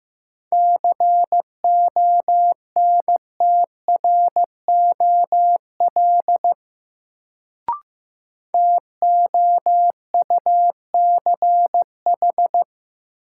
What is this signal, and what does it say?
Morse code, heard as a keyed tone, about 15 wpm, 700 Hz. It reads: CONTROL TOUCH
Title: Sentences from Top 1200 Words - New Word Drill Alt - 15wpm